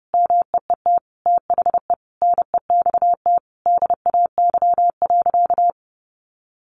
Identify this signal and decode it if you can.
Morse code, heard as a keyed tone, about 30 wpm, 700 Hz. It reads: MEET THE NEXT DAY.
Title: Sentences from Top 1400 Words 30wpm